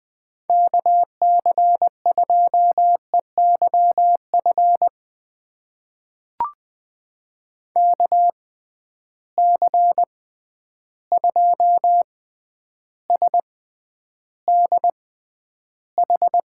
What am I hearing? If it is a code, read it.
Morse code, heard as a keyed tone, about 20 wpm, spaced as for 10 wpm, 700 Hz. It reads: KC2EYF KC2SDH
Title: USA Call Signs - 6 Characters - 6x Character Spacing - 20wpm